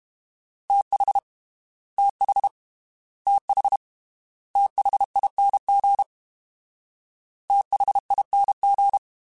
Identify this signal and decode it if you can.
Morse code, heard as a keyed tone, about 32 wpm, 800 Hz. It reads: TH TH TH THING THING
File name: TH / TH mp3s / TH Prefixes 3x2 Twice @30